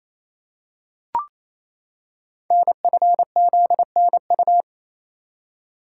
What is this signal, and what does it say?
Morse code, heard as a keyed tone, about 29 wpm, 700 Hz. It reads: NFZNU
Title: Code Groups - 5 Letters - 28wpm